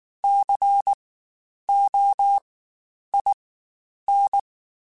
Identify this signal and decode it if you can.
Morse code, heard as a keyed tone, about 19 wpm, spaced as for 9 wpm, 800 Hz. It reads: COIN
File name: Spaced Alphas / Spaced Alphas mp3s / Cs sp @18